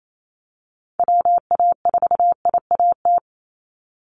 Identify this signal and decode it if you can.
Morse code, heard as a keyed tone, about 28 wpm, 700 Hz. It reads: WA4IAT
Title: USA Call Signs 28wpm